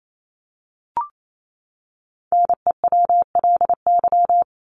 Morse code, heard as a keyed tone, about 28 wpm, 700 Hz. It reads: NEWLY
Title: LY Suffix - Top 100 Words with Base Words 28wpm